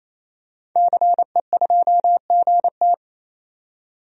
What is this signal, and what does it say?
Morse code, heard as a keyed tone, about 28 wpm, 700 Hz. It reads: CE2GT
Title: DXCC-100 Call Signs - 5 Characters - 28wpm